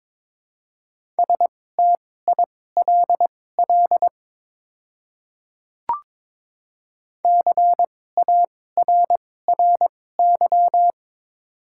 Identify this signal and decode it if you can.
Morse code, heard as a keyed tone, about 22 wpm, spaced as for 16 wpm, 700 Hz. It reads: STILL CARRY
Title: 2x Character Spacing - Top 300 Words - 22wpm